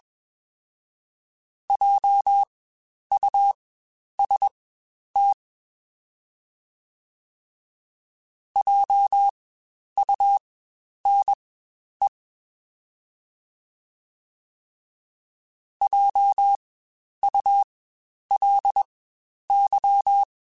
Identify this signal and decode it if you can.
Morse code, heard as a keyed tone, about 21 wpm, spaced as for 10 wpm, 800 Hz. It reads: JUST JUNE JULY
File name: Spaced Alphas / Spaced Alphas mp3s / Js sp @20